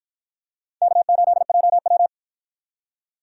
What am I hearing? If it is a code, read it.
Morse code, heard as a keyed tone, about 53 wpm, 700 Hz. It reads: K8JW